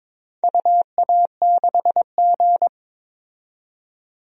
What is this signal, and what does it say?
Morse code, heard as a keyed tone, about 22 wpm, 700 Hz. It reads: UA6G
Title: DXCC-100 Call Signs - 4 Characters - 22wpm